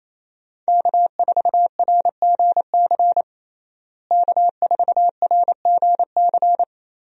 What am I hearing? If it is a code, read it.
Morse code, heard as a keyed tone, about 28 wpm, 700 Hz. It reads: K4RGC K4RGC
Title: USA Call Signs - 5 Characters - Repeated 2x - 28wpm